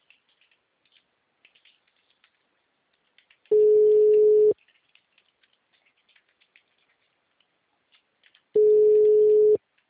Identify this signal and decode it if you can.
Ringback tone (Europe, 425 Hz).
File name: out- -2007- - - .89888